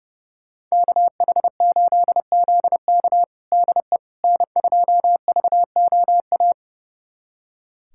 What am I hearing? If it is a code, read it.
Morse code, heard as a keyed tone, about 30 wpm, 700 Hz. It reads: KH8ZK DE N2VOA
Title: USA Callsign DE Callsign - 5 Characters - 30wpm